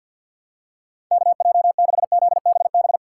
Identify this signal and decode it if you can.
Morse code, heard as a keyed tone, about 51 wpm, 700 Hz. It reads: KJ6ZBB
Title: USA Callsign DE Callsign 50wpm